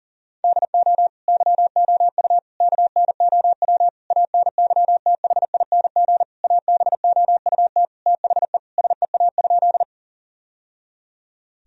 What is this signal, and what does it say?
Morse code, heard as a keyed tone, about 37 wpm, 700 Hz. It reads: DO YOU KNOW ANYTHING ABOUT THE SEA?
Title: Sentences from Top 1000 Words - Review Part 1 - 40wpm